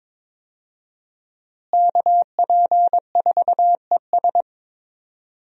Morse code, heard as a keyed tone, about 22 wpm, 700 Hz. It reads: KP4ES
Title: USA Call Signs - 5 Characters - Repeated 3x - 22wpm